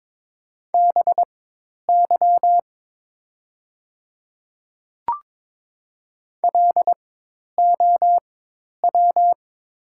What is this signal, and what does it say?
Morse code, heard as a keyed tone, about 22 wpm, spaced as for 10 wpm, 690 Hz. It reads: BY LOW
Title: 4x Character Spacing - Top 200 Words - 22wpm